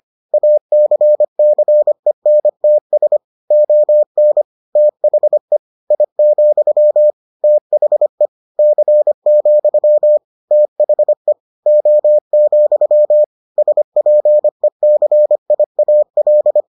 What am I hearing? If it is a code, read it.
Morse code, heard as a keyed tone, about 25 wpm, 600 Hz. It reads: ACCENTS ON THE I, THE C, THE O, SPECIAL